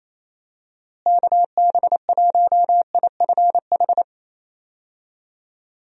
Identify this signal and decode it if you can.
Morse code, heard as a keyed tone, about 28 wpm, 700 Hz. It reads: KB1IFH
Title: USA Call Signs - 6 Characters - Repeated 3x - 28wpm